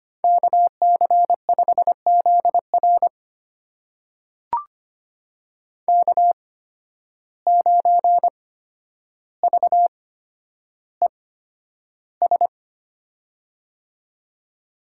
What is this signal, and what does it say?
Morse code, heard as a keyed tone, about 25 wpm, spaced as for 11 wpm, 700 Hz. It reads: KC5ZR K9VES
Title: USA Call Signs - 5 Characters - 8x Character Spacing - 25wpm